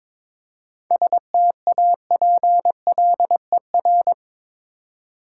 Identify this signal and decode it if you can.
Morse code, heard as a keyed tone, about 22 wpm, 700 Hz. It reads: STAPLER
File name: Difficult Words with Hints - 22wpm